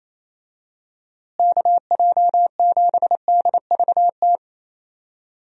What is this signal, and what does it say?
Morse code, heard as a keyed tone, about 28 wpm, 700 Hz. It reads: KJ7DVT
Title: USA Call Signs - 6 Characters - Repeated 3x - 28wpm